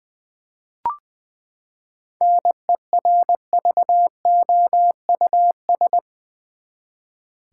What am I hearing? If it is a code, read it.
Morse code, heard as a keyed tone, about 20 wpm, 700 Hz. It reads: NERVOUS
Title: Seven Letter Words - 20wpm